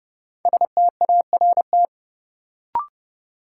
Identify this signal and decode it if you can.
Morse code, heard as a keyed tone, about 30 wpm, 700 Hz. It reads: START
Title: Sentences from Top 300 Words - New Word Drill - 30wpm